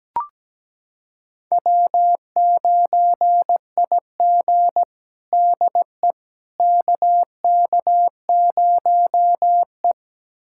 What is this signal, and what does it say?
Morse code, heard as a keyed tone, about 17 wpm, 700 Hz. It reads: W9IG DE KK0E
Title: USA Callsign DE Callsign - 4 Characters - 17wpm